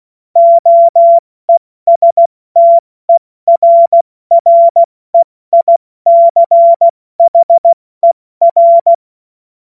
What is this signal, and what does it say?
Morse code, heard as a keyed tone, about 16 wpm, spaced as for 14 wpm, 680 Hz. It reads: OESTERREICHER
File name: Morsekurs / Witze / Witz 07.November 2023 70 F80